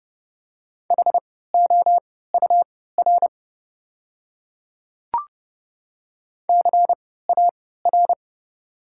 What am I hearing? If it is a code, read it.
Morse code, heard as a keyed tone, about 30 wpm, spaced as for 17 wpm, 690 Hz. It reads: HOUR CAR
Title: 3x Character Spacing - Top 500 Words - 30wpm